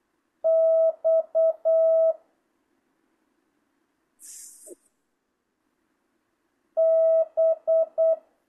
Morse code, heard as a keyed tone, about 8 wpm, 650 Hz. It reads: X B
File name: jh8cbh / 2025cw